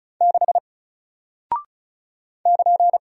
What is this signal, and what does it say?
Morse code, heard as a keyed tone, about 36 wpm, 700 Hz. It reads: 6 <KN>
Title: Single Character (Letter:Number:Punctuation:Prosign) Alt 35wpm